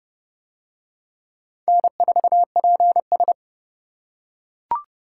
Morse code, heard as a keyed tone, about 26 wpm, 700 Hz. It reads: N4PS